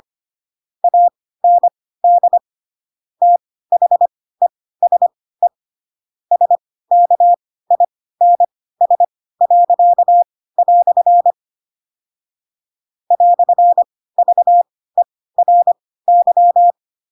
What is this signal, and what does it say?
Morse code, heard as a keyed tone, about 25 wpm, spaced as for 16 wpm, 700 Hz. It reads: AND THESE SKINS." "VERY
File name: Pinocchio / Pinocchio ch7 10 25 ik1hge